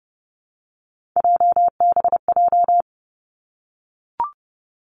Morse code, heard as a keyed tone, about 30 wpm, 700 Hz. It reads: JBJ
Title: Three Challenging Letter-Number Combinations 30wpm